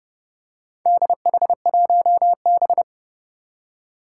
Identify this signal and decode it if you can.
Morse code, heard as a keyed tone, about 30 wpm, 700 Hz. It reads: DH1B
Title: Four Challenging Letter-Number Combinations 30wpm